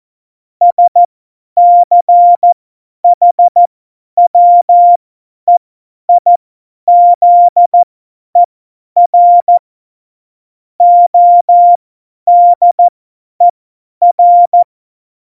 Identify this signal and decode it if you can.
Morse code, heard as a keyed tone, about 14 wpm, spaced as for 10 wpm, 700 Hz. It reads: SCHWEIZER ODER
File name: Morsekurs / Witze / Witz 29. August 17, Tempo 50 F70